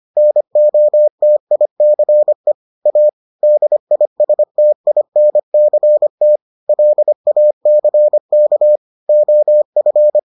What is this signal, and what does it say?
Morse code, heard as a keyed tone, about 25 wpm, 600 Hz. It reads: NOTICE A DISTINCT LACK OF